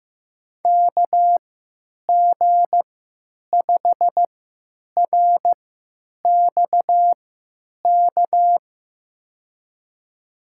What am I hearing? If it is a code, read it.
Morse code, heard as a keyed tone, about 15 wpm, spaced as for 9 wpm, 700 Hz. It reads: KG5RXK
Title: USA Call Signs - 6 Characters - 3x Character Spacing - 15wpm